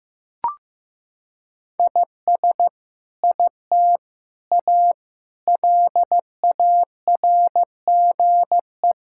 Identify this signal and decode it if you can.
Morse code, heard as a keyed tone, about 15 wpm, 700 Hz. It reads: IS IT A LARGE
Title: Sentences from Top 1000 Words - Review Part 5 - 15wpm